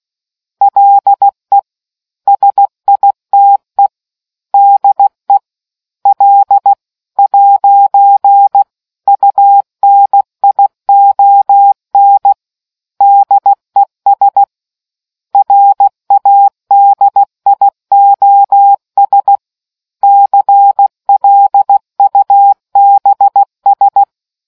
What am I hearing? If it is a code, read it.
Morse code, heard as a keyed tone, about 16 wpm, 800 Hz. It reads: LE SITE DE L'UNION DES RADIOS CLUBS